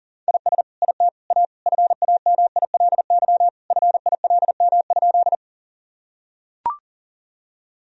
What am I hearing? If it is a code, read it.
Morse code, heard as a keyed tone, about 37 wpm, 700 Hz. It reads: IS IT A FAMILY FILM?
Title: Sentences from Top 1000 Words - Review Part 3 - 40wpm